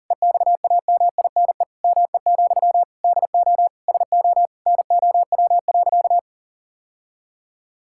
Morse code, heard as a keyed tone, about 40 wpm, 700 Hz. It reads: EXAMINE ME, DO SO NOW.
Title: Sentences from Top 1600 Words 40wpm